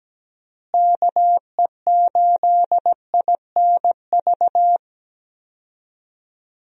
Morse code, heard as a keyed tone, about 17 wpm, 700 Hz. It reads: KE8INV
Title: USA Call Signs - 6 Characters - Repeated 3x - 17wpm